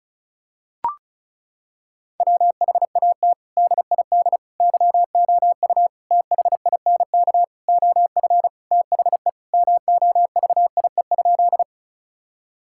Morse code, heard as a keyed tone, about 36 wpm, 700 Hz. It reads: WHAT DID YOU THINK OF THE MOVIE?
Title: Sentences from Top 1000 Words - Review Part 5 - 35wpm